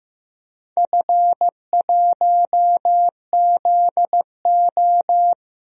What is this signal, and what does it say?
Morse code, heard as a keyed tone, about 15 wpm, 700 Hz. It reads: F1ZO
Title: DXCC-100 Call Signs - 4 Characters - Repeated 2x - 15wpm